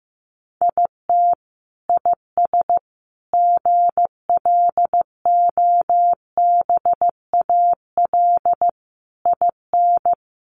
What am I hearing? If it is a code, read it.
Morse code, heard as a keyed tone, about 15 wpm, 700 Hz. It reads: IT IS GLOBAL IN
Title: Sentences from Top 1200 Words 15wpm